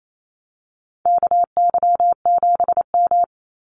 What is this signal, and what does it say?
Morse code, heard as a keyed tone, about 28 wpm, 700 Hz. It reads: KY7M
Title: Call Signs Repeated 3x - 28wpm